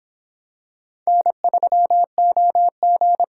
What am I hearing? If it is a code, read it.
Morse code, heard as a keyed tone, about 25 wpm, 700 Hz. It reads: N3OG